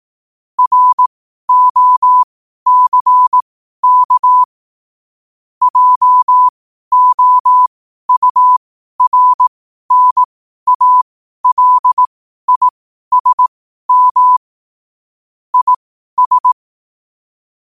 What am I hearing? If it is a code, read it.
Morse code, heard as a keyed tone, about 18 wpm, spaced as for 12 wpm, 1000 Hz. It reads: ROCK JOURNALISM IS
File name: qotdfiles / 12wpm